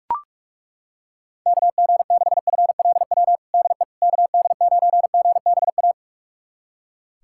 Morse code, heard as a keyed tone, about 42 wpm, 700 Hz. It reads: KG6FLW DE KD9GBA
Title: USA Callsign DE Callsign 45wpm